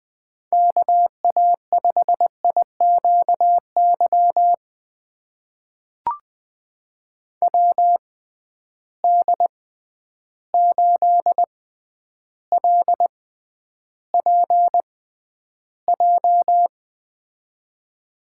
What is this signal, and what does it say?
Morse code, heard as a keyed tone, about 20 wpm, spaced as for 10 wpm, 700 Hz. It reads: KA5IQY WD8LPJ
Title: USA Call Signs - 6 Characters - 6x Character Spacing - 20wpm